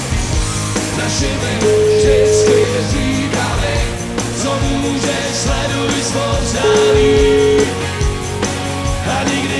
Ringback tone (Europe, 425 Hz), with noise.